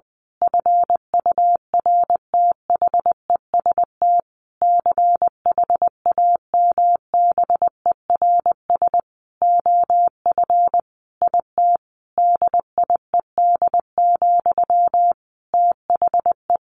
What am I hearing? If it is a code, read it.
Morse code, heard as a keyed tone, about 20 wpm, 700 Hz. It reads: FURTHEST CHAMBERS OF IT DIED, THE
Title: A Sound Of Thunder 20wpm